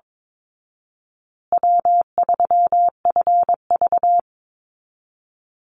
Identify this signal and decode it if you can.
Morse code, heard as a keyed tone, about 22 wpm, 700 Hz. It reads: W3FV
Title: Easy Call Signs Repeated 3x - 22wpm